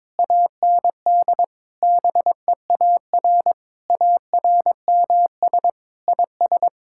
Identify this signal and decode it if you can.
Morse code, heard as a keyed tone, about 21 wpm, 700 Hz. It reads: AND BEAR ARMS IS
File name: Sentences with 9-Word Phrases - 22wpm